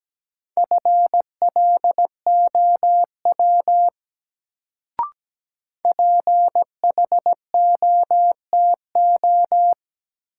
Morse code, heard as a keyed tone, about 17 wpm, 700 Hz. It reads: FLOW PHOTO
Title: Sentences from Top 1200 Words - New Word Drill Alt - 17wpm